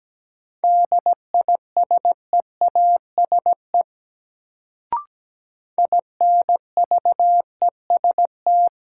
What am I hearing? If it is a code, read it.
Morse code, heard as a keyed tone, about 17 wpm, 700 Hz. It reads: DISEASE INVEST
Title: Sentences from Top 1100 Words - New Word Drill Alt - 17wpm